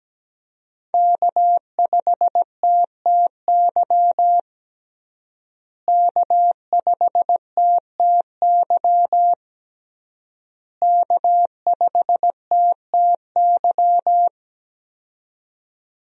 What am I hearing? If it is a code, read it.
Morse code, heard as a keyed tone, about 17 wpm, 700 Hz. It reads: K5TTY K5TTY K5TTY
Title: USA Call Signs - 5 Characters - Repeated 3x - 17wpm